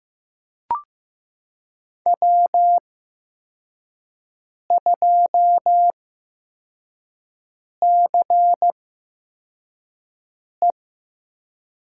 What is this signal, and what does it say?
Morse code, heard as a keyed tone, about 15 wpm, spaced as for 4 wpm, 700 Hz. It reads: W2CE
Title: USA Call Signs - 5 Characters - 8x Character Spacing - 15wpm